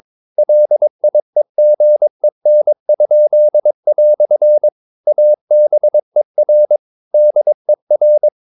Morse code, heard as a keyed tone, about 22 wpm, 600 Hz. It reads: LIEGEN?" ABER DER